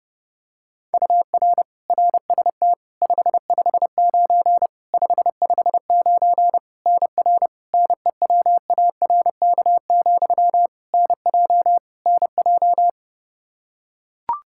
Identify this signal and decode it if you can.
Morse code, heard as a keyed tone, about 32 wpm, 700 Hz. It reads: UR RST 559 559 NR NEWARK, NJ NJ
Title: K1USN SST QSO Practice 30wpm